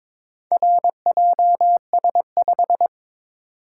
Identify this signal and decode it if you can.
Morse code, heard as a keyed tone, about 22 wpm, 690 Hz. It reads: RJS5
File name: practice / 22wpm / Four Letter-Number Combinations - 22wpm